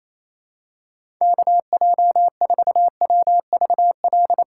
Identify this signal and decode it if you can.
Morse code, heard as a keyed tone, about 28 wpm, 700 Hz. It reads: KJ4WVL
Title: USA Call Signs - 6 Characters - 4x Character Spacing - 28wpm